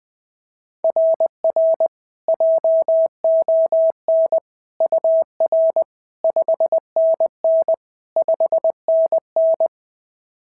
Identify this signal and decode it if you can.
Morse code, heard as a keyed tone, about 20 wpm, 690 Hz. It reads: RR JON UR 5NN 5NN
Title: SOTA QSO Practice 20wpm